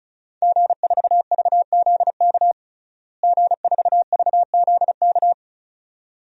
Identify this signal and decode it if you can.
Morse code, heard as a keyed tone, about 36 wpm, 700 Hz. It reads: G4VZK G4VZK
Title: DXCC-100 Call Signs - 5 Characters - Repeated 2x - 35wpm